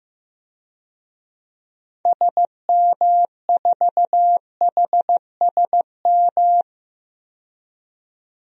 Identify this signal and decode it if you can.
Morse code, heard as a keyed tone, about 15 wpm, 700 Hz. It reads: SM4HSM
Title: DXCC-100 Call Signs - 6 Characters - 15wpm